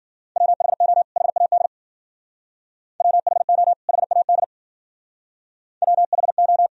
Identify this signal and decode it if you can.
Morse code, heard as a keyed tone, about 50 wpm, 700 Hz. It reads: WHO HAD WHO HAD WHO
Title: Speed-Racing - Sets of 2 Words 40wpm